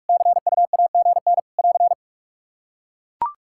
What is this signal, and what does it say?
Morse code, heard as a keyed tone, about 44 wpm, 700 Hz. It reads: CUAGN <AR>
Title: Practice QSO - Exchange 4 - Ending - 45wpm